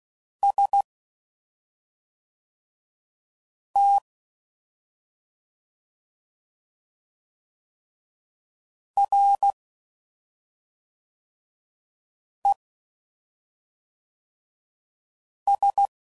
Morse code, heard as a keyed tone, about 16 wpm, 800 Hz. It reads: S T R E S
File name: Basic Letter Groups By7Groups / LFCYQ / LFCYQ@5 f15